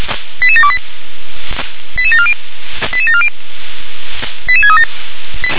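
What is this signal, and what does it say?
Five-tone selective call (ZVEI), heard as quick runs of five tones, tones of about 70 ms. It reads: 88728 88740 88740 88637
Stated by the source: ZVEI five-tone call received by SDR.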